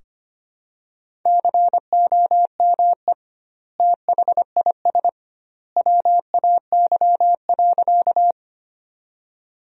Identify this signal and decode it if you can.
Morse code, heard as a keyed tone, about 25 wpm, 700 Hz. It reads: COME THIS WAY.
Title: 2x Word-Spacing - Sentences from Top 100 Words - 25wpm